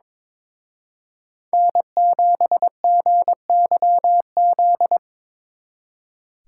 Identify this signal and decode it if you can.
Morse code, heard as a keyed tone, about 26 wpm, 700 Hz. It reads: N7GYZ
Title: Speed Racing - USA 1x3 Call Signs 22wpm